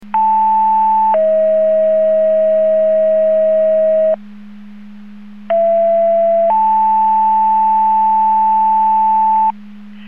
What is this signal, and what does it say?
Two-tone page, heard as a one-second tone, then a three-second tone, about 880 then 640 Hz, about 670 then 880 Hz. It is heard from 0.1 to 9.5 s.